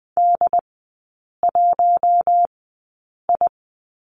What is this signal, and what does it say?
Morse code, heard as a keyed tone, about 20 wpm, spaced as for 8 wpm, 700 Hz. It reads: D1I
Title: 067 Lesson 12e - Letter d - Three Characters 20wpm